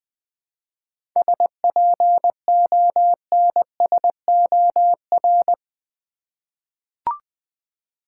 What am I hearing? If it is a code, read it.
Morse code, heard as a keyed tone, about 21 wpm, 700 Hz. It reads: SPONSOR
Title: 7 Letter Words - First 3 Letters Given - 20wpm